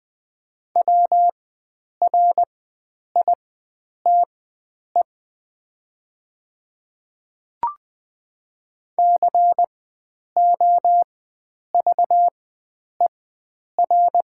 Morse code, heard as a keyed tone, about 20 wpm, spaced as for 9 wpm, 700 Hz. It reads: WRITE COVER
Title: 4x Character Spacing - Top 300 Words - 20wpm